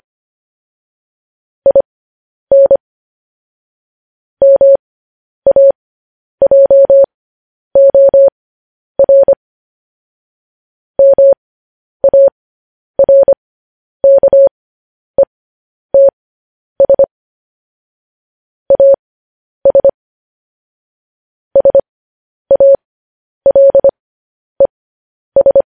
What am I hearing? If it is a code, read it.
Morse code, heard as a keyed tone, about 25 wpm, spaced as for 10 wpm, 550 Hz. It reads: IN MAJOR MARKETS AS SALES
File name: podcast / News headlines-1x-25x10